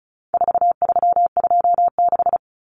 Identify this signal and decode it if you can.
Morse code, heard as a keyed tone, about 34 wpm, 700 Hz. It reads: 4326
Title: Common Cut Numbers - 4-Digit 35wpm